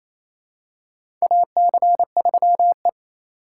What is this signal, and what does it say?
Morse code, heard as a keyed tone, about 28 wpm, 700 Hz. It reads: AC3E
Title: USA Call Signs - 4 Characters - 2x Character Spacing - 28wpm